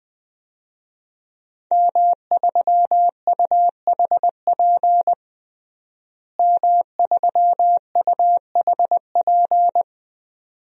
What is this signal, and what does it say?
Morse code, heard as a keyed tone, about 21 wpm, 700 Hz. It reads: M3UHP M3UHP
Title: DXCC-100 Call Signs - 5 Characters - Repeated 2x - 20wpm